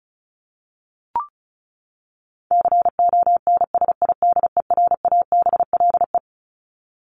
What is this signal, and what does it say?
Morse code, heard as a keyed tone, about 34 wpm, 700 Hz. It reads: CONSIDERABLE
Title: LY Suffix - Top 100 Words with Base Words 35wpm